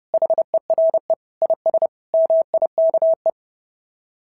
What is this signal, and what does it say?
Morse code, heard as a keyed tone, about 32 wpm, 660 Hz. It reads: HERE IS MIKE
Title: SOTA QSO Practice 30wpm